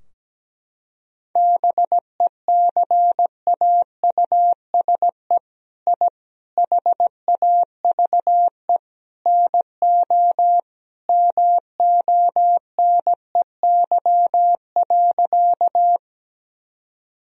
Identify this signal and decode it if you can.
Morse code, heard as a keyed tone, about 17 wpm, 700 Hz. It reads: BECAUSE I HAVE NO MONEY.
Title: Sentences from Top 1000 Words - Review Part 3 - 17wpm